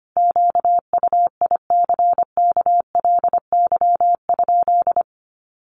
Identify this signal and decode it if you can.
Morse code, heard as a keyed tone, about 25 wpm, 700 Hz. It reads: QUICKLY?
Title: Sentences from Top 700 Words 25wpm